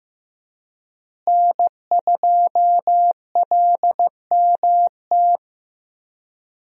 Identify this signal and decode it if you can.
Morse code, heard as a keyed tone, about 15 wpm, 700 Hz. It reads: N2LMT
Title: USA Call Signs - 5 Characters - Repeated 3x - 15wpm